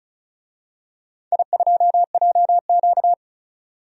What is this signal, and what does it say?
Morse code, heard as a keyed tone, about 36 wpm, 700 Hz. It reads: I2JQ
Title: DXCC-100 Call Signs - 4 Characters - 35wpm